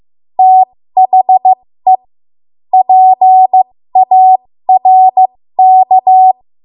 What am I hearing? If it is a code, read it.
Morse code, heard as a keyed tone, about 15 wpm, spaced as for 13 wpm, 750 Hz. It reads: THE PARK
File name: Archive / 13 WPM / 13WPM